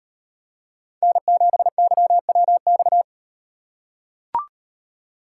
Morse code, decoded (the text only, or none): N7YWX